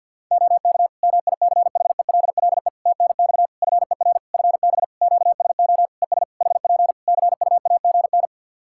Morse code, heard as a keyed tone, about 50 wpm, 700 Hz. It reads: OK MICHELLE TNX FER FB QSO ES HP CUAGN
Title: Practice QSO - Exchange 4 - Ending - 50wpm